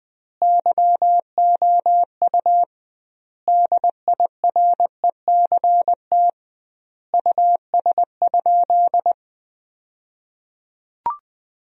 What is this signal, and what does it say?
Morse code, heard as a keyed tone, about 20 wpm, 700 Hz. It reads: YOU DIRECT US?
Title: 2x Word-Spacing - Sentences from Top 400 Words - 20wpm